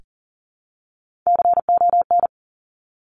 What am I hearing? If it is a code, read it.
Morse code, heard as a keyed tone, about 40 wpm, 700 Hz. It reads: CON
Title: Top Three Letter Combinations in Words 40wpm